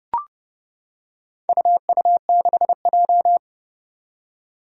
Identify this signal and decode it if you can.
Morse code, heard as a keyed tone, about 31 wpm, 700 Hz. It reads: UU6J
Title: Four Challenging Letter-Number Combinations 30wpm